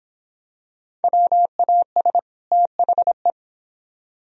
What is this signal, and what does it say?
Morse code, heard as a keyed tone, about 25 wpm, 700 Hz. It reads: WAS THE